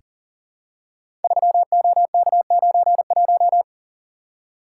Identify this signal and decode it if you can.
Morse code, heard as a keyed tone, about 40 wpm, 700 Hz. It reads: 3OK91
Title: Code Groups - 5 Letter:Numbers - 40wpm